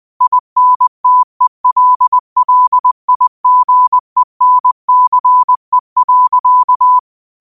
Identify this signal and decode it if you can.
Morse code, heard as a keyed tone, about 20 wpm, 1000 Hz. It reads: INTELLIGENCE.
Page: Quotes for Fri, 15 Aug 2025 in Morse Code at 20 words per minute.